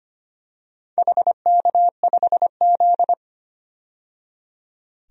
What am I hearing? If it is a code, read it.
Morse code, heard as a keyed tone, about 25 wpm, 700 Hz. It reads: HK5Z
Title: DXCC-100 Call Signs - 4 Characters - 25wpm